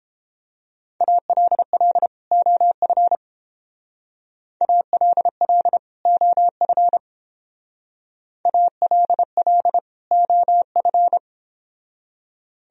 Morse code, heard as a keyed tone, about 29 wpm, 700 Hz. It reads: ALL OF ALL OF ALL OF